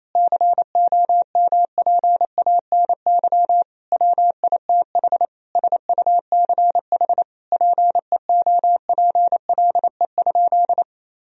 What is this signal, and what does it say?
Morse code, heard as a keyed tone, about 28 wpm, 700 Hz. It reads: COMPANY WITH SUCH PEOPLE?
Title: Sentences from Top 1000 Words - Review Part 3 - 28wpm